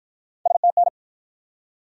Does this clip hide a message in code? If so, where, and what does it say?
Yes: Morse code from 0.5 to 0.9 s, heard as a keyed tone, about 48 wpm, 700 Hz. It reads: STN